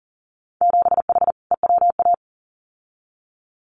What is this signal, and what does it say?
Morse code, heard as a keyed tone, about 40 wpm, 700 Hz. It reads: 7H EWA
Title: Winter Field Day Exchanges Encore 40wpm